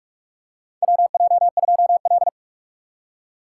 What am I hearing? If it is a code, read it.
Morse code, heard as a keyed tone, about 44 wpm, 700 Hz. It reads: WJ2L
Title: K1USN SST QSO Practice - 45wpm